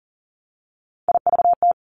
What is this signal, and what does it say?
Morse code, heard as a keyed tone, about 42 wpm, 700 Hz. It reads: IVT
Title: Three Challenging Letter-Number Combinations 40wpm